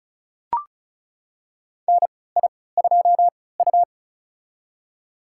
Morse code, heard as a keyed tone, about 36 wpm, spaced as for 20 wpm, 700 Hz. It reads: NI2U
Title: USA Call Signs - 4 Characters - 3x Character Spacing - 35wpm